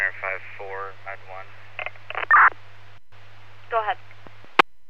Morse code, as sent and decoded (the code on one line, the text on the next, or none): none